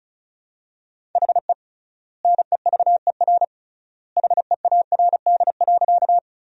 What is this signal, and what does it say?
Morse code, heard as a keyed tone, about 34 wpm, 700 Hz. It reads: HE NEVER HEARD.
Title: 3x Word-Spacing - Sentences from Top 400 Words - 35wpm